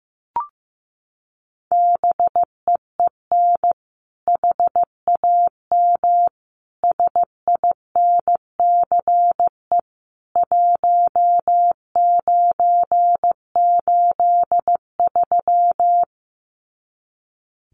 Morse code, heard as a keyed tone, about 14 wpm, 700 Hz. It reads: BEEN HAM SINCE 1983
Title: Common QSO Phrases 15wpm